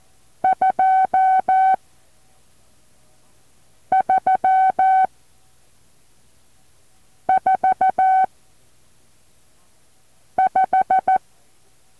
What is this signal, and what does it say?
Morse code, heard as a keyed tone, about 14 wpm, 760 Hz. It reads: 2 3 4 5
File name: intl morse code2